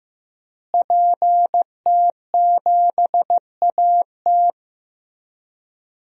Morse code, heard as a keyed tone, about 15 wpm, 700 Hz. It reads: PT7AT
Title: DXCC-100 Call Signs - 5 Characters - Repeated 2x - 15wpm